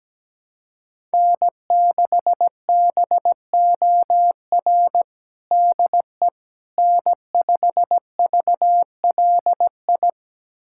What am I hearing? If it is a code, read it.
Morse code, heard as a keyed tone, about 17 wpm, 700 Hz. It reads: N6BOR DE N5VLI
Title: USA Callsign DE Callsign - 5 Characters - 17wpm